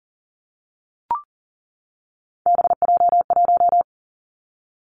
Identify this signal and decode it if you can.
Morse code, heard as a keyed tone, about 42 wpm, 690 Hz. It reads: BJ1
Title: Three Challenging Letter-Number Combinations 40wpm